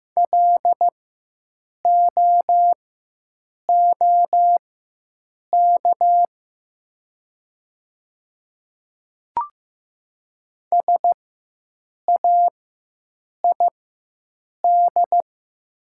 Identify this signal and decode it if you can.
Morse code, heard as a keyed tone, about 15 wpm, spaced as for 7 wpm, 700 Hz. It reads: LOOK SAID
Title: 4x Character Spacing - Top 300 Words - 15wpm